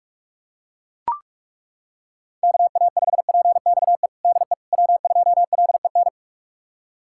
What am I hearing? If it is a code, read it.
Morse code, heard as a keyed tone, about 42 wpm, 700 Hz. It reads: KA5PXE DE W2LEN
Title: USA Callsign DE Callsign 45wpm